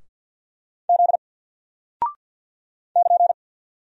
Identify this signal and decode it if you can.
Morse code, heard as a keyed tone, about 50 wpm, 700 Hz. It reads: Z <KN>
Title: Single Character (Letter:Number:Punctuation:Prosign) Alt 50wpm